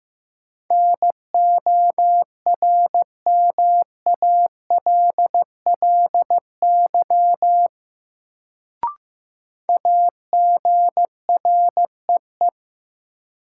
Morse code, heard as a keyed tone, about 15 wpm, 700 Hz. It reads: NORMALLY AGREE
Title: Sentences from Top 1000 Words - Review Part 6 - New Word Drill - 15wpm